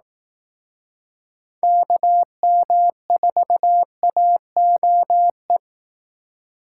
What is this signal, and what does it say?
Morse code, heard as a keyed tone, about 18 wpm, 700 Hz. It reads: KM4AOE
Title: Speed Racing - USA 2x3 Call Signs 15wpm